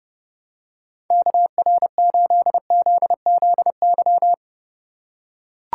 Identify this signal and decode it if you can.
Morse code, heard as a keyed tone, about 30 wpm, 700 Hz. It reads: KR8ZZY
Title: USA Call Signs - 6 Characters - 3x Character Spacing - 30wpm